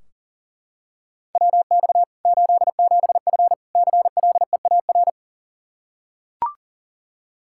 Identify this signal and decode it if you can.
Morse code, heard as a keyed tone, about 37 wpm, 700 Hz. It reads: WX 87F CLEAR
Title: Practice QSO - Exchange 2 - Info - 40wpm